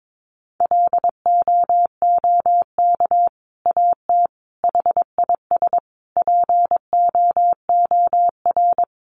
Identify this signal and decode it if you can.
Morse code, heard as a keyed tone, about 22 wpm, 700 Hz. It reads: LOOK AT HIS POOR
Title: Sentences from Top 1300 Words 22wpm